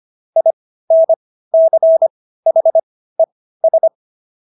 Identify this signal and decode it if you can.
Morse code, heard as a keyed tone, about 25 wpm, spaced as for 15 wpm, 650 Hz. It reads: INCHES
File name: SS110 15